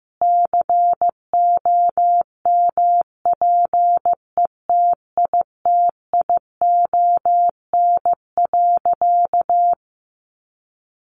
Morse code, heard as a keyed tone, about 15 wpm, 700 Hz. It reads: COMPETITION.
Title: Additional Easy Phrases 15wpm